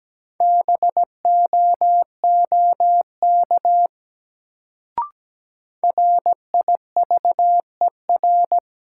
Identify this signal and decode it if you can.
Morse code, heard as a keyed tone, about 17 wpm, 700 Hz. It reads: BOOK RIVER
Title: Sentences from Top 300 Words - New Word Drill - 17wpm